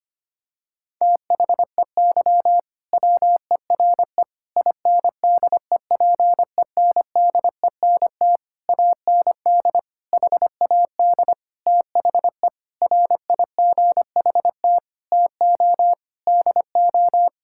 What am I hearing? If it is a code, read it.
Morse code, heard as a keyed tone, about 25 wpm, 700 Hz. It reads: THEY WERE INDEPENDENT AND HAD THE RIGHT TO DO
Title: Sentences from Top 1000 Words - Review Part 6 - 25wpm